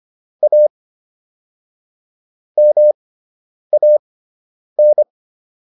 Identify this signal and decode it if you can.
Morse code, heard as a keyed tone, about 25 wpm, spaced as for 9 wpm, 600 Hz. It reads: A MAN